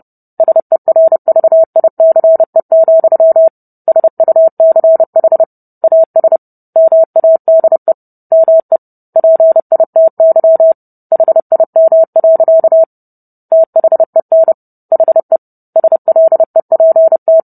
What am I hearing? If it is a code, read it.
Morse code, heard as a keyed tone, about 30 wpm, 650 Hz. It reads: SERVICE, SUCH AS MADE ME PITY HIM. THEN HE SLEPT